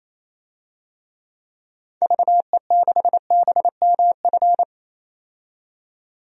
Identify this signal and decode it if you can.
Morse code, heard as a keyed tone, about 28 wpm, 700 Hz. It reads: VE6BMF
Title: DXCC-100 Call Signs - 6 Characters - 28wpm